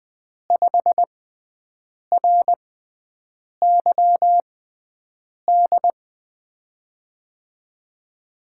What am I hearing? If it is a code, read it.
Morse code, heard as a keyed tone, about 20 wpm, spaced as for 7 wpm, 700 Hz. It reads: 5RYD
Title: USA Call Signs - 6x Character Spacing - 20wpm